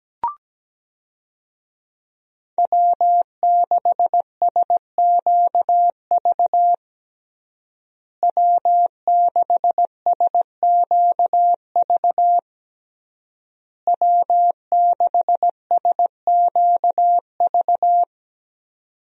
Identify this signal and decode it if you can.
Morse code, heard as a keyed tone, about 17 wpm, 700 Hz. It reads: W6SQV W6SQV W6SQV
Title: USA Call Signs - 5 Characters - Repeated 3x - 17wpm